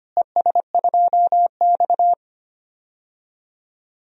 Morse code, heard as a keyed tone, about 25 wpm, 700 Hz. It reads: ES2X
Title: DXCC-100 Call Signs - 4 Characters - Repeated 2x - 25wpm